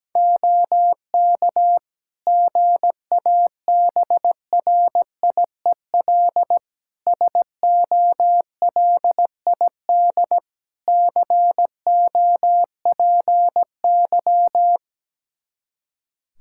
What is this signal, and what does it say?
Morse code, heard as a keyed tone, about 17 wpm, 700 Hz. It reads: OK GABRIEL SOLID COPY
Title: Practice QSO - Exchange 3 - More Info - 17wpm